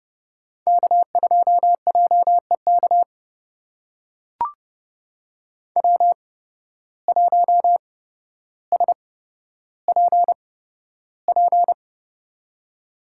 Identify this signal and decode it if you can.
Morse code, heard as a keyed tone, about 30 wpm, spaced as for 13 wpm, 700 Hz. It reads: K2JEK W1SPP
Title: USA Call Signs - 5 Characters - 8x Character Spacing - 30wpm